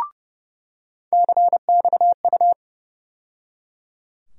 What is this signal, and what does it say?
Morse code, heard as a keyed tone, about 30 wpm, 700 Hz. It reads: CXU
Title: USA Call Sign Postfixes - 3 Letters 30wpm